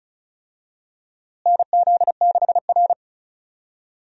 Morse code, heard as a keyed tone, about 36 wpm, 700 Hz. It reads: NZ6R
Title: USA Call Signs - 4 Characters - Repeated 2x - 35wpm